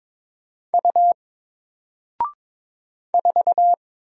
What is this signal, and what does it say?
Morse code, heard as a keyed tone, about 22 wpm, 700 Hz. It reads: U 4
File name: Single-Letter-Number-22wpm